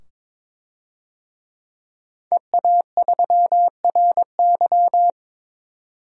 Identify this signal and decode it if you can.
Morse code, heard as a keyed tone, about 22 wpm, 700 Hz. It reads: EA3RY
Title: DXCC-100 Call Signs - 5 Characters - 22wpm